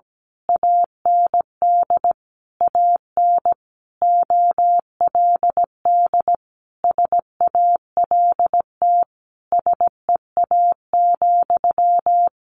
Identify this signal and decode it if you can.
Morse code, heard as a keyed tone, about 17 wpm, 700 Hz. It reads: AND AN OLD SALT SEA,
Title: A Sound Of Thunder 17wpm